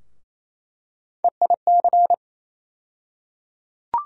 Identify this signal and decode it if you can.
Morse code, heard as a keyed tone, about 28 wpm, 700 Hz. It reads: EIC
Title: USA Call Sign Postfixes - 3 Letters 28wpm